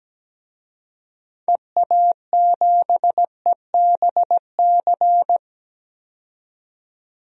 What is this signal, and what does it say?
Morse code, heard as a keyed tone, about 17 wpm, 700 Hz. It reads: EA7EBC
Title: DXCC-100 Call Signs - 6 Characters - 17wpm